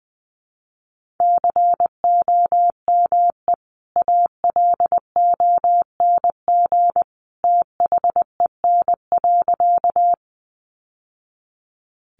Morse code, heard as a keyed tone, about 19 wpm, 700 Hz. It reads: COME ALONG THEN.
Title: Sentences from Top 700 Words 20wpm